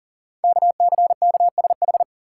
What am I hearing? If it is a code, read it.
Morse code, heard as a keyed tone, about 42 wpm, 690 Hz. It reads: KCKSH
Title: Code Groups - 5 Letters - 40wpm